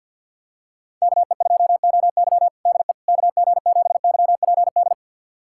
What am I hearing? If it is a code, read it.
Morse code, heard as a keyed tone, about 46 wpm, 700 Hz. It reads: KE2OY DE KG7YPD
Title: USA Callsign DE Callsign 50wpm